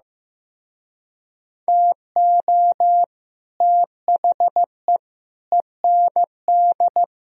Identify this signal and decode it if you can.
Morse code, heard as a keyed tone, about 16 wpm, 700 Hz. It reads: TO THE END
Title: Speed-Racing - Sets of 4 Words 15wpm